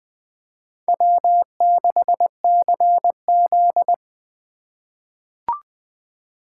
Morse code, heard as a keyed tone, about 20 wpm, 700 Hz. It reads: W6CZ
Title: USA Call Signs - 4 Characters - 4x Character Spacing - 20wpm